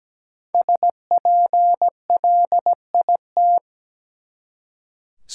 Morse code, heard as a keyed tone, about 17 wpm, 700 Hz. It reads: SPLIT
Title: Five Letter Words - 17wpm